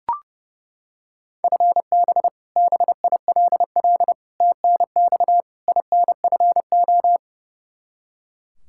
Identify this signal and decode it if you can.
Morse code, heard as a keyed tone, about 30 wpm, 700 Hz. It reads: FB BILL TNX INFO
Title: Practice QSO - Exchange 3 - More Info - 30wpm